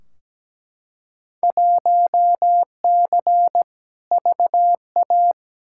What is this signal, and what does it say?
Morse code, heard as a keyed tone, about 17 wpm, 700 Hz. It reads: 1C VA
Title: Field Day Exchanges Encore - 17wpm